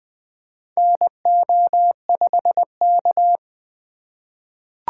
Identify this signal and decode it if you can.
Morse code, heard as a keyed tone, about 20 wpm, 700 Hz. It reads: NO5K
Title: USA Call Signs - 4 Characters - 2x Character Spacing - 20wpm